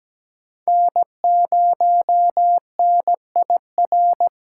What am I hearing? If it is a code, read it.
Morse code, heard as a keyed tone, about 20 wpm, 700 Hz. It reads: N0NIR
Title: Speed Racing - USA Call Signs 17wpm